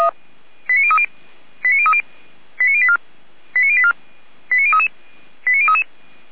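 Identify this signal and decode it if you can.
Five-tone selective call (ZVEI), heard as quick runs of five tones, tones of about 70 ms. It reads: 89030 89030 89084 89084 89033 89033
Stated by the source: ZVEI five-tone selective call 89140, received by SDR, from the Feuerwehr Orlamünde fire brigade audio server.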